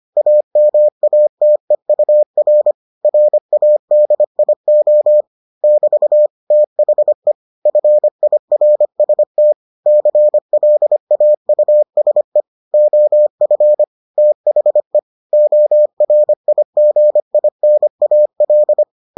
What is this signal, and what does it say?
Morse code, heard as a keyed tone, about 25 wpm, 600 Hz. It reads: AMATEUR RADIO <BT> THE FIRST CLAUSE OF THE ORIGINAL